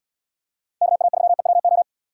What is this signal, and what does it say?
Morse code, heard as a keyed tone, about 72 wpm, 700 Hz. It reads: KA3EWO